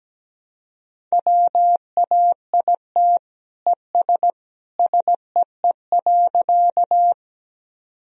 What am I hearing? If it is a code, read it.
Morse code, heard as a keyed tone, about 17 wpm, 700 Hz. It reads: WAIT ES SEE.
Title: Sentences from Top 500 Words and Abbreviations 17wpm